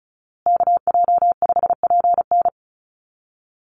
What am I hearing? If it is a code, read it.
Morse code, heard as a keyed tone, about 34 wpm, 700 Hz. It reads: KJ5PN
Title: USA Call Signs Repeated 2x 35wpm